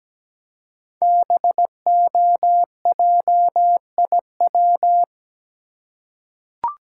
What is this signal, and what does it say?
Morse code, heard as a keyed tone, about 17 wpm, 700 Hz. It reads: BOJIW
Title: Code Groups - 5 Letters - 17wpm